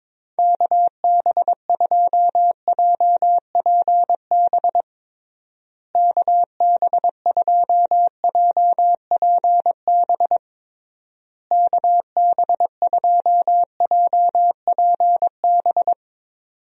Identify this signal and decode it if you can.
Morse code, heard as a keyed tone, about 22 wpm, 700 Hz. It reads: KB2JPB KB2JPB KB2JPB
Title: USA Call Signs - 6 Characters - Repeated 3x - 22wpm